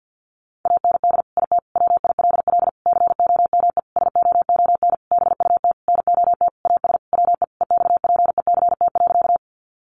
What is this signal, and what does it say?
Morse code, heard as a keyed tone, about 50 wpm, 700 Hz. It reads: AND IT WILL COME SOON BUT NOT AS WE EXPECT.
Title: Sentences from Top 1100 Words 50wpm